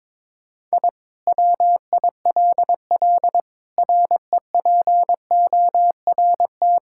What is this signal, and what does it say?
Morse code, heard as a keyed tone, about 21 wpm, 700 Hz. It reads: I WILL REPORT
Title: Sentences from Top 1000 Words - Review Part 6 - 22wpm